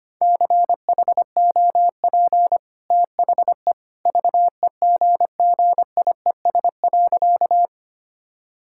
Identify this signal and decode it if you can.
Morse code, heard as a keyed tone, about 25 wpm, 690 Hz. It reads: CHOP THE VEGGIES.
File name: Easy Sentences with Hints - 25wpm